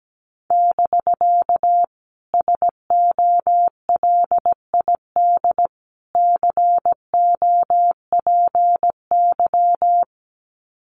Morse code, heard as a keyed tone, about 17 wpm, 700 Hz. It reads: <BK> SOLID COPY
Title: Common QSO Phrases Encore 17wpm